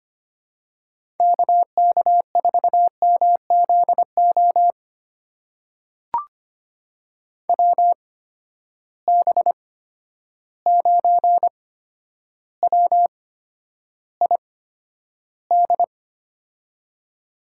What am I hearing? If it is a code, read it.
Morse code, heard as a keyed tone, about 25 wpm, spaced as for 11 wpm, 700 Hz. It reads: KK4MZO WB9WID
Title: USA Call Signs - 6 Characters - 8x Character Spacing - 25wpm